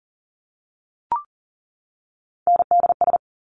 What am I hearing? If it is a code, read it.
Morse code, heard as a keyed tone, about 41 wpm, 700 Hz. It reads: NDS
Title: Top Three Letter Combinations in Words 40wpm